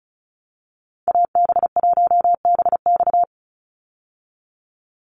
Morse code, heard as a keyed tone, about 34 wpm, 700 Hz. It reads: AB1BX
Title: Call Signs Repeated 3x - 35wpm